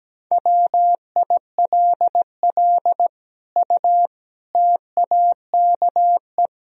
Morse code, heard as a keyed tone, about 17 wpm, 700 Hz. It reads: WILL U TAKE
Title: Sentences from Top 200 Words and Abbreviations 17wpm